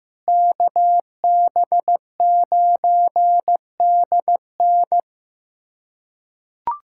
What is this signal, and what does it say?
Morse code, heard as a keyed tone, about 16 wpm, 700 Hz. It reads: KB9DN
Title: USA Callsign DE Callsign - 5 Characters - 15wpm